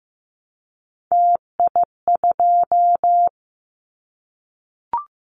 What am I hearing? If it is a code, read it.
Morse code, heard as a keyed tone, about 15 wpm, 700 Hz. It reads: TI2
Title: Callsign Prefixes for Top 100 DXCC - 15wpm